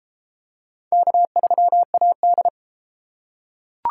Morse code, heard as a keyed tone, about 31 wpm, 700 Hz. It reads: K3AD